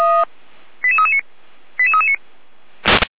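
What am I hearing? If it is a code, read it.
Five-tone selective call (ZVEI), heard as quick runs of five tones, tones of about 70 ms. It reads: 88339 88339
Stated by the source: ZVEI five-tone selective call, address 88341, received by SDR at feuerwehr-orlamuende (fire brigade Orlamünde).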